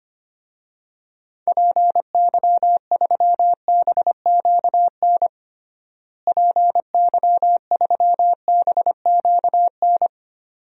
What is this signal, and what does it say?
Morse code, heard as a keyed tone, about 25 wpm, 700 Hz. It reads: PY3BQN PY3BQN
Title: DXCC-100 Call Signs - 6 Characters - Repeated 2x - 25wpm